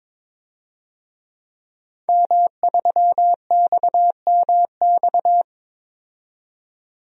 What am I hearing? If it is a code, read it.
Morse code, heard as a keyed tone, about 22 wpm, 700 Hz. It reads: M3XMX
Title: DXCC-100 Call Signs - 5 Characters - 22wpm